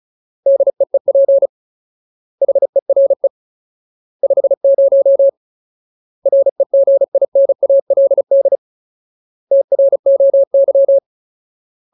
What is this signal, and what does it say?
Morse code, heard as a keyed tone, about 36 wpm, 550 Hz. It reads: DEEP HERE 50 REGINALD TROY
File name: Ultimate Word List / 35WPM Ultimate